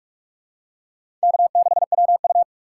Morse code, heard as a keyed tone, about 41 wpm, 700 Hz. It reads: K6WU